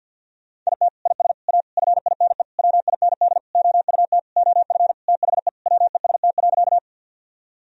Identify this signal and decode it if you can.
Morse code, heard as a keyed tone, about 50 wpm, 700 Hz. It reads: IT IS A FINE WIND OUT OF THE WEST.
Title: Sentences from Top 1000 Words - Review Part 2 - 50wpm